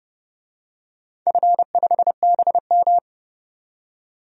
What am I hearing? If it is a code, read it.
Morse code, heard as a keyed tone, about 30 wpm, 700 Hz. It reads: F5BM
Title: DXCC-100 Call Signs - 4 Characters - 30wpm